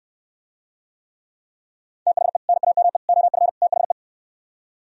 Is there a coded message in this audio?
Yes: Morse code from 2.1 to 3.9 s, heard as a keyed tone, about 69 wpm, 700 Hz. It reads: THE NAME OF THE